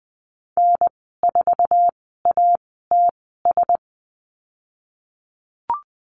Morse code, heard as a keyed tone, about 20 wpm, spaced as for 15 wpm, 700 Hz. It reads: N4ATS
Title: 088 Lesson 15g - Number 5 - Call Signs 20wpm